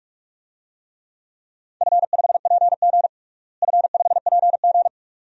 Morse code, heard as a keyed tone, about 41 wpm, 700 Hz. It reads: F5PG F5PG
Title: DXCC-100 Call Signs - 4 Characters - Repeated 2x - 45wpm